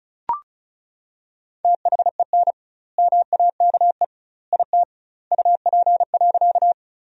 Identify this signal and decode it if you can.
Morse code, heard as a keyed tone, about 34 wpm, 700 Hz. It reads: THEN MAKE IT UP.
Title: 2x Word-Spacing - Sentences from Top 100 Words - 35wpm